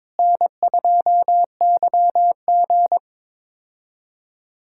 Morse code, heard as a keyed tone, about 22 wpm, 700 Hz. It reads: N2YG
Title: K1USN SST QSO Practice - 22wpm